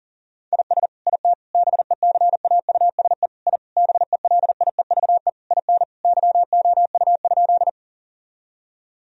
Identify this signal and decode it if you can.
Morse code, heard as a keyed tone, about 40 wpm, 700 Hz. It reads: IS IT BECAUSE I BELIEVE IN YOU?
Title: Sentences from Top 1000 Words - Review Part 3 - 40wpm